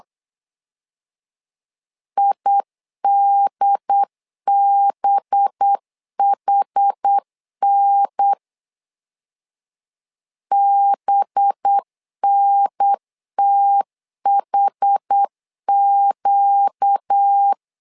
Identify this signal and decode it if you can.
Morse code, heard as a keyed tone, about 8 wpm, 800 Hz. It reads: IDBHN BNTHQ
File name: LEZIONI / 5 LBVC / 5 LB 8wpm 2816